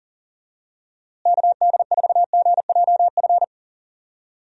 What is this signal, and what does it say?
Morse code, heard as a keyed tone, about 37 wpm, 700 Hz. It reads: KD4GJF
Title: USA Call Signs - 6 Characters - Repeated 3x - 40wpm